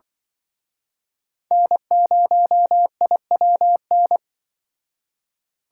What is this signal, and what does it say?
Morse code, heard as a keyed tone, about 25 wpm, 700 Hz. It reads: N0IWN